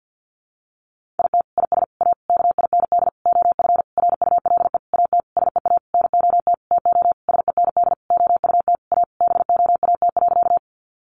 Morse code, heard as a keyed tone, about 50 wpm, 700 Hz. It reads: IT IS A KIND OF RULE AT SEA NOT TO SEND OUT A BOAT.
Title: Sentences from Top 1500 Words 50wpm